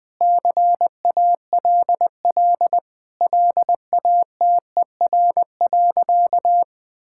Morse code, heard as a keyed tone, about 20 wpm, 700 Hz. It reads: CALL LATER.
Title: Sentences with 4-Word Phrases - 20wpm